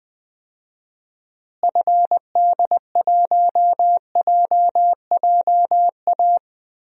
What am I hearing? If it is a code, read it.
Morse code, heard as a keyed tone, about 20 wpm, 700 Hz. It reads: FD1JJA
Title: DXCC-100 Call Signs - 6 Characters - Repeated 3x - 20wpm